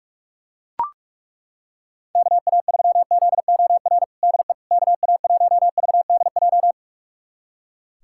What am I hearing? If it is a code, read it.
Morse code, heard as a keyed tone, about 42 wpm, 700 Hz. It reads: KA3ZOR DE KA1VDJ
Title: USA Callsign DE Callsign - 6 Characters - 45wpm